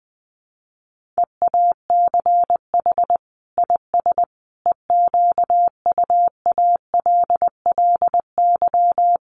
Morse code, heard as a keyed tone, about 20 wpm, 700 Hz. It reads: EACH IS EQUALLY
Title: Sentences from Top 1400 Words 20wpm